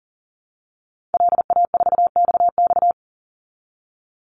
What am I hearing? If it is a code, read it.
Morse code, heard as a keyed tone, about 38 wpm, 700 Hz. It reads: LA4XX
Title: Call Signs Repeated 3x - 40wpm